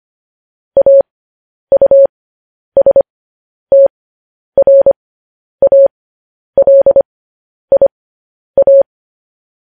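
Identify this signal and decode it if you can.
Morse code, heard as a keyed tone, about 25 wpm, spaced as for 10 wpm, 550 Hz. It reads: AUSTRALIA
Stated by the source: Morse code news headlines at 10 wpm.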